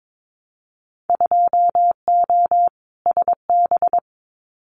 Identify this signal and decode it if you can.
Morse code, heard as a keyed tone, about 22 wpm, 700 Hz. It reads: 2O SB
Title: Winter Field Day Exchanges 22wpm